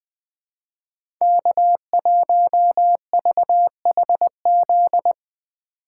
Morse code, heard as a keyed tone, about 20 wpm, 700 Hz. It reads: K1VHZ
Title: USA Call Signs - 3x Character Spacing - 20wpm